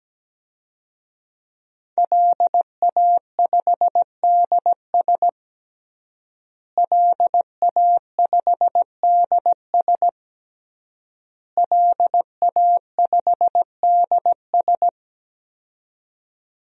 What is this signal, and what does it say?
Morse code, heard as a keyed tone, about 17 wpm, 700 Hz. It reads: LA5DS LA5DS LA5DS
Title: DXCC-100 Call Signs - 5 Characters - Repeated 3x - 17wpm